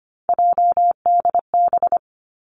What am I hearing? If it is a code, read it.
Morse code, heard as a keyed tone, about 25 wpm, 700 Hz. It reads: JDB
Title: Three Challenging Letter-Number Combinations 25wpm